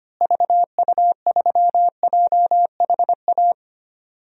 Morse code, heard as a keyed tone, about 25 wpm, 700 Hz. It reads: VU3JHA
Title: DXCC-100 Call Signs - 6 Characters - Repeated 2x - 25wpm